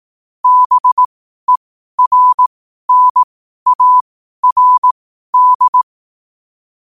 Morse code, heard as a keyed tone, about 18 wpm, spaced as for 12 wpm, 1000 Hz. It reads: BERNARD
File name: qotdfiles / 12wpm